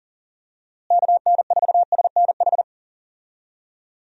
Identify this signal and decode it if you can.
Morse code, heard as a keyed tone, about 40 wpm, 700 Hz. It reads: KN4SNH
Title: USA Call Signs - 6 Characters - Repeated 2x - 40wpm